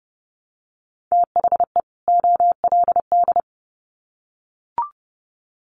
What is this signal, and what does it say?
Morse code, heard as a keyed tone, about 30 wpm, 700 Hz. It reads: THE OLD
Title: Sets of 2 Words 30wpm